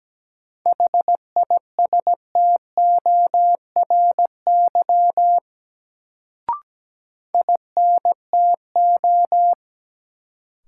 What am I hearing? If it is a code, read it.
Morse code, heard as a keyed tone, about 17 wpm, 700 Hz. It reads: HISTORY INTO
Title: Sentences from Top 600 Words - New Word Drill - 17wpm